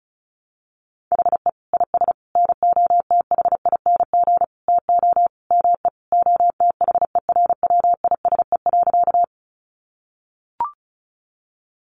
Morse code, heard as a keyed tone, about 34 wpm, 700 Hz. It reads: HE IS NOTHING TO ME OTHERWISE.
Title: Sentences from Top 1200 Words 35wpm